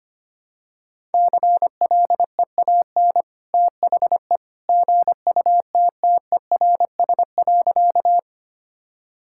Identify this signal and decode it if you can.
Morse code, heard as a keyed tone, about 25 wpm, 700 Hz. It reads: CLEAN THE GUTTERS.
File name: Easy Sentences with Hints - 25wpm